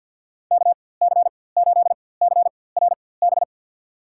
Morse code, heard as a keyed tone, about 53 wpm, spaced as for 24 wpm, 700 Hz. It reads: KC8CRB
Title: USA Call Signs - 6 Characters - 4x Character Spacing - 50wpm